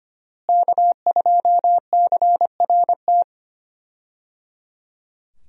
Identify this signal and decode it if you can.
Morse code, heard as a keyed tone, about 25 wpm, 700 Hz. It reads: K2CRT
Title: USA Call Signs - 5 Characters - Repeated 3x - 25wpm